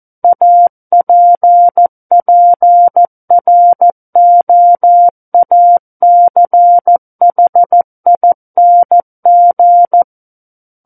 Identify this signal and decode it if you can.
Morse code, heard as a keyed tone, about 14 wpm, 700 Hz. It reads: APPROACHING
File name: Book One / 05-Chapter 5 - 14 WPM